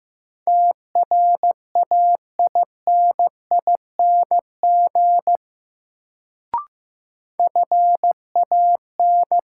Morse code, heard as a keyed tone, about 15 wpm, 700 Hz. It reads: TRAINING FAN
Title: Sentences from Top 1000 Words - Review Part 4 - New Word Drill - 15wpm